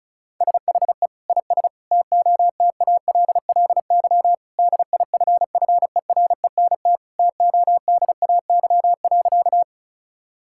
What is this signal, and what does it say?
Morse code, heard as a keyed tone, about 36 wpm, 700 Hz. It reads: SHE IS TOTALLY DIFFERENT TODAY.
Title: Sentences from Top 1600 Words 35wpm